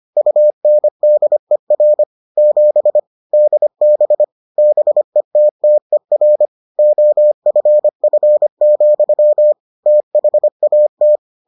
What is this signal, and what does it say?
Morse code, heard as a keyed tone, about 25 wpm, 600 Hz. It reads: UNDER 7 DB BETTER OFF, THAT